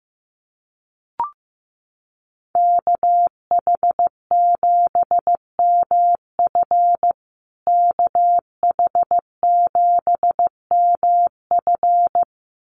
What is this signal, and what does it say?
Morse code, heard as a keyed tone, about 15 wpm, 700 Hz. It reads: KH7MF KH7MF
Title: USA Call Signs Repeated 2x 15wpm